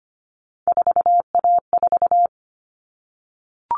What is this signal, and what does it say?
Morse code, heard as a keyed tone, about 25 wpm, 700 Hz. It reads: 4A4
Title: All Cut Numbers - 3-Digit 25wpm